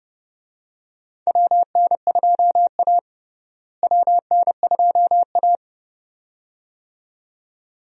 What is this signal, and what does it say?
Morse code, heard as a keyed tone, about 29 wpm, 700 Hz. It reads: WN2A WN2A
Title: USA Call Signs - 4 Characters - Repeated 2x - 30wpm